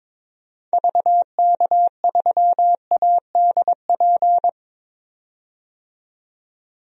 Morse code, heard as a keyed tone, about 22 wpm, 700 Hz. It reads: VK3ADP
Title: DXCC-100 Call Signs - 6 Characters - 22wpm